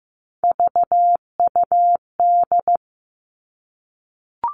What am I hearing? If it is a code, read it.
Morse code, heard as a keyed tone, about 15 wpm, 700 Hz. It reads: VUD
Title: Three Challenging Letter-Number Combinations 15wpm